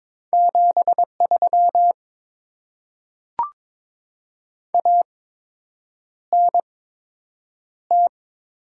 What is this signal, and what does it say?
Morse code, heard as a keyed tone, about 22 wpm, spaced as for 8 wpm, 700 Hz. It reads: 73 ANT
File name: Top 20 QSO Elements - 8x Character Spacing - 22wpm